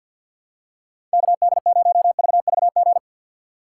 Morse code, heard as a keyed tone, about 46 wpm, 700 Hz. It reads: KD0VVG
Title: USA Call Signs - 6 Characters - 6x Character Spacing - 50wpm